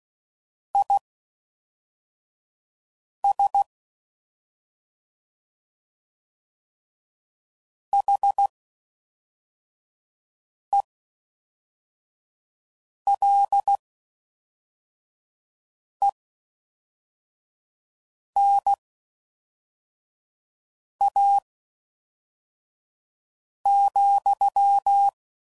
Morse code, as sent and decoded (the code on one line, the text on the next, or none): .. ... / .... .-.. -. .- --..--
IS HELENA,